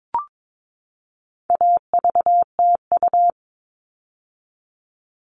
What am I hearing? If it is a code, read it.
Morse code, heard as a keyed tone, about 22 wpm, 700 Hz. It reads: AVTU
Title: All Cut Numbers - 4-Digit 22wpm